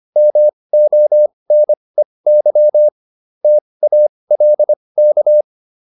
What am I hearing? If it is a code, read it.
Morse code, heard as a keyed tone, about 25 wpm, spaced as for 20 wpm, 600 Hz. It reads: MONEY TALK